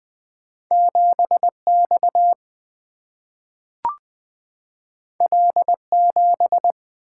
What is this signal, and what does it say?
Morse code, heard as a keyed tone, about 20 wpm, 700 Hz. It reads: 7X L7
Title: Single-Letter Single-Number 20wpm